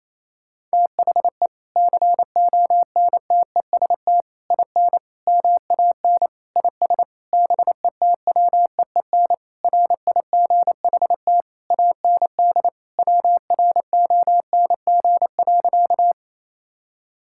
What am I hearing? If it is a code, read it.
Morse code, heard as a keyed tone, about 28 wpm, 700 Hz. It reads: THE CONTEST IN MAN IS BETWEEN RIGHT AND WRONG.
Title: Sentences from Top 1600 Words 28wpm